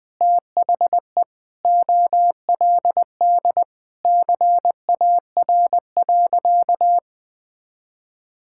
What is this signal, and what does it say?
Morse code, heard as a keyed tone, about 21 wpm, 700 Hz. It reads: THE OLD CAR.
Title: Sentences with 8-Word Phrases - Spoken Beginning - 20wpm